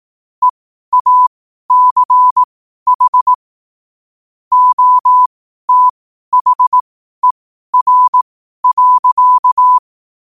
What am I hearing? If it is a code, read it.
Morse code, heard as a keyed tone, about 18 wpm, spaced as for 12 wpm, 1000 Hz. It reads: EACH OTHER.
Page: New quotes every day in morse code at 12 Words per minute.